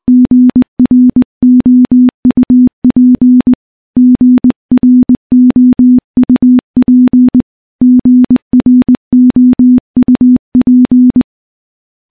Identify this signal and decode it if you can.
Morse code, heard as a keyed tone, about 20 wpm, 250 Hz. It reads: GROUP GROUP GROUP
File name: morse-training-20WPM